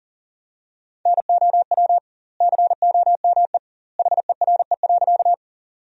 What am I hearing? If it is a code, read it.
Morse code, heard as a keyed tone, about 40 wpm, 700 Hz. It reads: NOW COME HERE.
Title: 2x Word-Spacing - Sentences from Top 200 Words - 40wpm